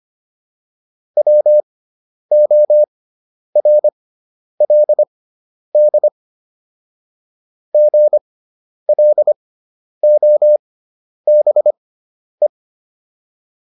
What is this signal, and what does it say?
Morse code, heard as a keyed tone, about 25 wpm, spaced as for 10 wpm, 600 Hz. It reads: WORLD GLOBE